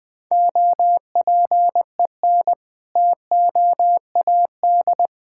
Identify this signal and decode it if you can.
Morse code, heard as a keyed tone, about 21 wpm, 700 Hz. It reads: OPEN TOAD
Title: All-Age Appropriate Jokes - 20wpm